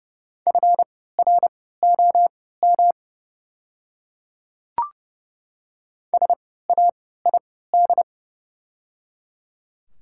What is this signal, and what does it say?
Morse code, heard as a keyed tone, about 30 wpm, spaced as for 17 wpm, 700 Hz. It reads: FROM SAID
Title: 3x Character Spacing - Top 200 Words - 30wpm